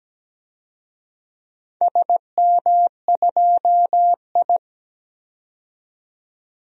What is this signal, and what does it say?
Morse code, heard as a keyed tone, about 17 wpm, 700 Hz. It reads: SM2I
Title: DXCC-100 Call Signs - 4 Characters - 17wpm